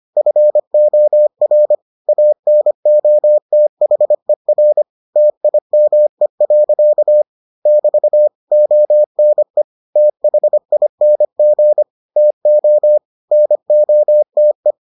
Morse code, heard as a keyed tone, about 25 wpm, 600 Hz. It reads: FOR ANOTHER TIME. <BT> ONE THING TO NOTE